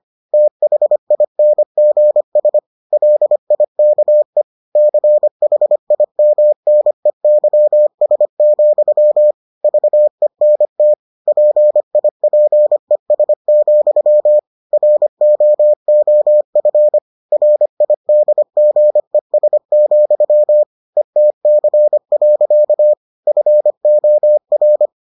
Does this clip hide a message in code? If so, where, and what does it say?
Yes: Morse code from 0.3 to 24.9 s, heard as a keyed tone, about 25 wpm, 600 Hz. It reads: THINGS LIKE CHIMNEYS, VENT PIPES, ROOF RIDGES, ETC. FOR